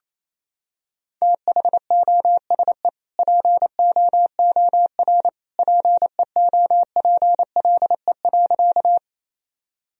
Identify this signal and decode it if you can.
Morse code, heard as a keyed tone, about 28 wpm, 700 Hz. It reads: THOSE POOR PEOPLE.
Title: Sentences from Top 1000 Words - Review Part 1 - 28wpm